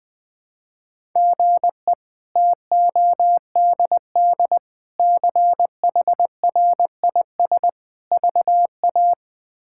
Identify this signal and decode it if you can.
Morse code, heard as a keyed tone, about 20 wpm, 700 Hz. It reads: GE TODD CHRIS VA
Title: K1USN SST QSO Practice - 20wpm